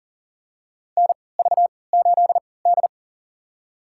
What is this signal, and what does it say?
Morse code, heard as a keyed tone, about 40 wpm, spaced as for 23 wpm, 700 Hz. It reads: NV8D
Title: USA Call Signs - 4 Characters - 3x Character Spacing - 40wpm